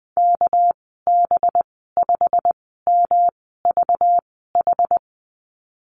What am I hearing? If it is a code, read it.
Morse code, heard as a keyed tone, about 19 wpm, spaced as for 14 wpm, 700 Hz. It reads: KB5MVH
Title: 200 Lesson 32g - Letter k - Call Signs 20wpm